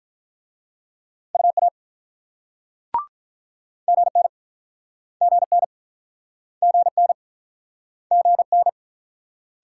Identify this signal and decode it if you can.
Morse code, heard as a keyed tone, about 43 wpm, 700 Hz. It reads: UA GN GN GN GN
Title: Speed-Racing Multi-letter Phonemes - 35wpm